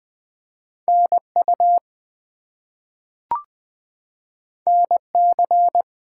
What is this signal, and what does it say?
Morse code, heard as a keyed tone, about 20 wpm, 700 Hz. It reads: NU NC
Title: US States and Canadian Provinces 20wpm